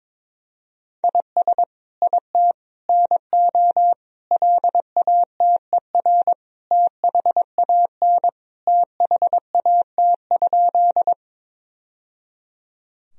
Morse code, heard as a keyed tone, about 22 wpm, 700 Hz. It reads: IS IT NO LATER THAN THAT?
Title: Sentences from Top 1000 Words - Review Part 4 - 22wpm